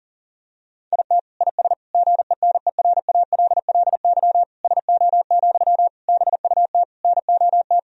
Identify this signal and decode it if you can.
Morse code, heard as a keyed tone, about 37 wpm, 700 Hz. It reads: IT IS GENERALLY SO, BUT NOT
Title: Sentences from Top 1000 Words - Review Part 4 - 40wpm